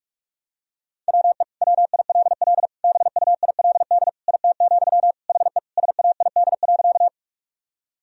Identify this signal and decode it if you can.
Morse code, heard as a keyed tone, about 42 wpm, 700 Hz. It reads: WE WILL BUILD IT, HE SAID.
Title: Sentences from Top 1000 Words - Review Part 1 - 45wpm